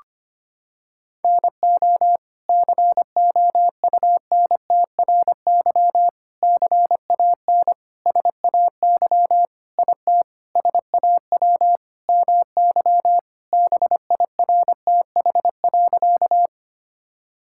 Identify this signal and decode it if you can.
Morse code, heard as a keyed tone, about 25 wpm, 700 Hz. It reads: NO COUNTRY CAN SAY IT SAW MY BIRTH.
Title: Sentences from Top 1000 Words - Review Part 5 - 25wpm